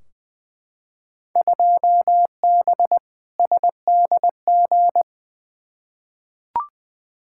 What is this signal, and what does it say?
Morse code, heard as a keyed tone, about 21 wpm, 700 Hz. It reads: 2B SDG
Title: Field Day Exchanges Encore - 20wpm